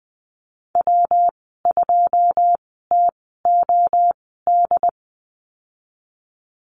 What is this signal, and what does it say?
Morse code, heard as a keyed tone, about 20 wpm, spaced as for 14 wpm, 700 Hz. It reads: W2TOD
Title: 140 Lesson 23g - Letter f - Call Signs 20wpm